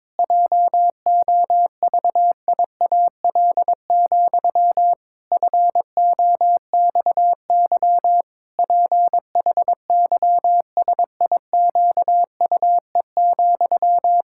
Morse code, heard as a keyed tone, about 22 wpm, 700 Hz. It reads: JOVIAL, FOXY PHYSIQUE,
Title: Pangram Challenge v2 - 22wpm